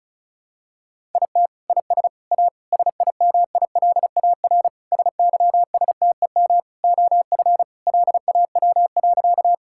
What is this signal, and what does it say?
Morse code, heard as a keyed tone, about 36 wpm, 700 Hz. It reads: IT IS A SIMILAR SYSTEM OF LAW.
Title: Sentences from Top 1000 Words - Review Part 5 - 35wpm